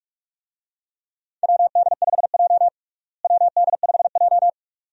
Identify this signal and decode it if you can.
Morse code, heard as a keyed tone, about 42 wpm, 700 Hz. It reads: WD5J WD5J
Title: USA Call Signs - 4 Characters - Repeated 2x - 45wpm